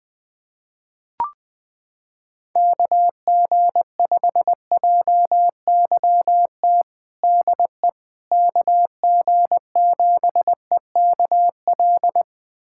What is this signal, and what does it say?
Morse code, heard as a keyed tone, about 20 wpm, 700 Hz. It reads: KG5JYT DE KG7EKL
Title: USA Callsign DE Callsign 20wpm